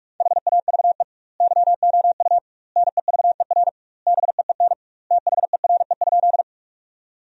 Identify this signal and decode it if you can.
Morse code, heard as a keyed tone, about 46 wpm, 700 Hz. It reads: HAVE YOU NEVER BEEN THERE?
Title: 2x Word-Spacing - Sentences from Top 300 Words - 45wpm